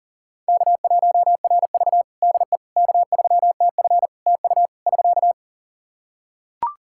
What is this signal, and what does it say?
Morse code, heard as a keyed tone, about 37 wpm, 700 Hz. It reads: K1RV DE K3TF TU <SK>
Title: Practice QSO - Exchange 4 - Ending - 40wpm